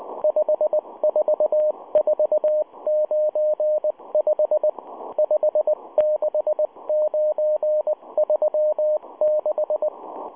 Morse code, heard as a keyed tone, about 19 wpm, 610 Hz, with radio noise. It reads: 54495 56936